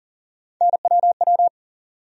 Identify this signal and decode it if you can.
Morse code, heard as a keyed tone, about 40 wpm, 700 Hz. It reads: NWW
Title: USA Call Sign Postfixes - 3 Letters 40wpm